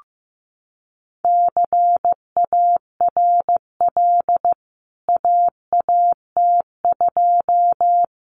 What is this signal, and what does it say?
Morse code, heard as a keyed tone, about 15 wpm, 700 Hz. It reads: CARL AAT2
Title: CWT Exchange 15wpm